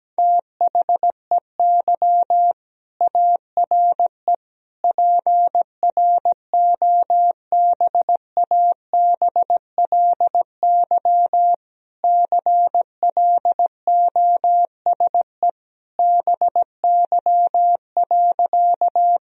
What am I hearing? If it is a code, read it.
Morse code, heard as a keyed tone, about 17 wpm, 700 Hz. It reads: THEY ARE PROBABLY CLOSE BY.
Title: Sentences from Top 1000 Words - Review Part 4 - 17wpm